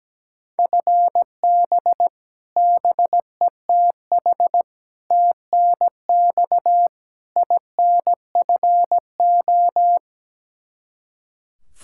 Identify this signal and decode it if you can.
Morse code, heard as a keyed tone, about 17 wpm, 700 Hz. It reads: FB BETH TNX INFO
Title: Practice QSO - Exchange 3 - More Info - 17wpm